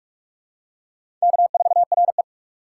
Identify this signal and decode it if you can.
Morse code, heard as a keyed tone, about 46 wpm, 700 Hz. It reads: K4RE
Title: USA Call Signs - 4 Characters - 3x Character Spacing - 45wpm